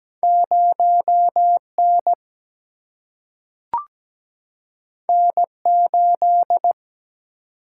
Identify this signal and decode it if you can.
Morse code, heard as a keyed tone, about 17 wpm, 700 Hz. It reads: 0N N8
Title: Single-Letter Single-Number 17wpm